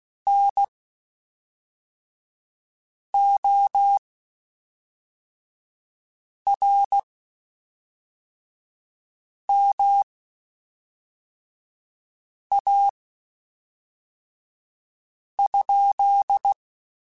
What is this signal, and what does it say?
Morse code, heard as a keyed tone, about 16 wpm, spaced as for 3 wpm, 800 Hz. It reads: NORMA?
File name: (spaced) / QSOs Spaced-mp3s / WD0FBT 1 @5 sp